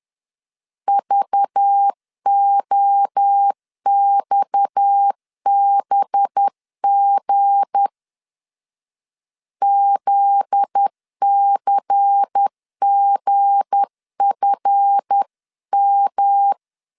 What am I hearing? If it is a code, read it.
Morse code, heard as a keyed tone, about 10 wpm, 800 Hz. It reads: VOXBG ZCGFM
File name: LEZIONI / 7 GF9Z / 7 GF9Z 10wpm 2611